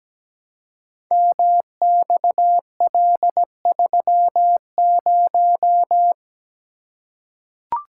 Morse code, heard as a keyed tone, about 17 wpm, 700 Hz. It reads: MXL30
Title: Code Groups - 5 Letter:Numbers - 17wpm